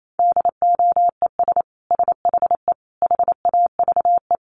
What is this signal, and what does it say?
Morse code, heard as a keyed tone, about 28 wpm, 700 Hz. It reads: DOES SHE HAVE
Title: Sentences from Top 1400 Words 28wpm